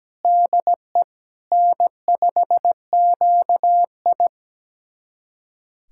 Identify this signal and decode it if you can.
Morse code, heard as a keyed tone, about 17 wpm, 700 Hz. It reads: DE N5QI
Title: USA Callsign DE Callsign - 4 Characters - 17wpm